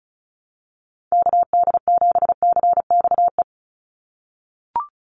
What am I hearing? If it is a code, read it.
Morse code, heard as a keyed tone, about 34 wpm, 700 Hz. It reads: KD7CXE
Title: USA Call Signs 35wpm